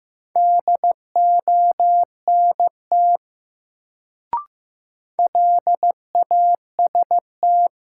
Morse code, heard as a keyed tone, about 15 wpm, 700 Hz. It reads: DONT LAST
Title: Sentences from Top 300 Words - New Word Drill - 15wpm